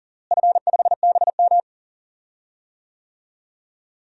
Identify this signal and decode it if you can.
Morse code, heard as a keyed tone, about 40 wpm, 700 Hz. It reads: F5BM
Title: DXCC-100 Call Signs - 4 Characters - Repeated 3x - 40wpm